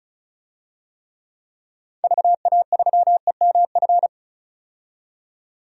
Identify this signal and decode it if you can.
Morse code, heard as a keyed tone, about 36 wpm, 700 Hz. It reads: VA3EMF
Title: DXCC-100 Call Signs - 6 Characters - 35wpm